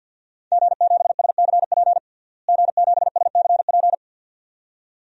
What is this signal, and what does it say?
Morse code, heard as a keyed tone, about 47 wpm, 700 Hz. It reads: G7SCP G7SCP
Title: DXCC-100 Call Signs - 5 Characters - Repeated 2x - 50wpm